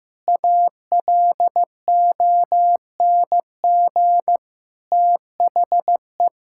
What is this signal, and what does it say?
Morse code, heard as a keyed tone, about 15 wpm, 700 Hz. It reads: ALONG THE
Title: Sentences from Top 1000 Words - Review Part 6 - 15wpm